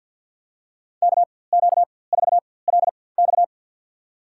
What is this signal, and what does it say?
Morse code, heard as a keyed tone, about 52 wpm, spaced as for 23 wpm, 700 Hz. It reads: KQ4LX
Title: USA Call Signs - 4x Character Spacing - 50wpm